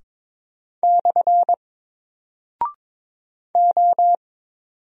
Morse code, heard as a keyed tone, about 22 wpm, 700 Hz. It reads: / O
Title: Single Character (Letter:Number:Punctuation:Prosign) 22wpm